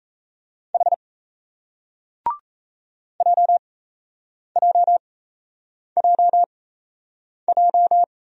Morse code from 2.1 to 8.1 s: JJJJ

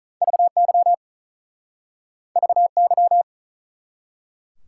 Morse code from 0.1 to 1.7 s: VY

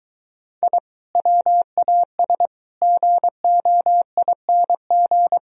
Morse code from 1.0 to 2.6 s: WAS

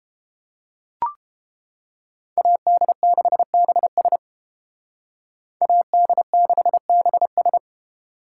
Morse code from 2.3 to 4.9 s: AD6BS

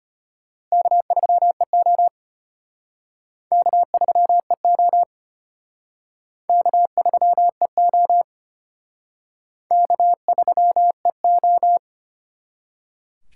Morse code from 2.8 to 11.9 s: K3EO K3EO K3EO

